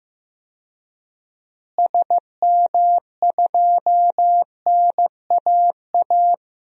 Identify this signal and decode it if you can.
Morse code, heard as a keyed tone, about 15 wpm, 700 Hz. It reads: SM2NAA
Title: DXCC-100 Call Signs - 6 Characters - Repeated 3x - 15wpm